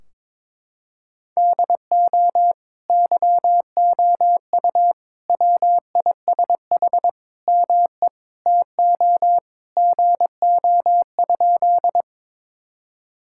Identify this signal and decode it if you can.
Morse code, heard as a keyed tone, about 22 wpm, 700 Hz. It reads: DO YOU WISH ME TO GO?
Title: Sentences from Top 1600 Words 22wpm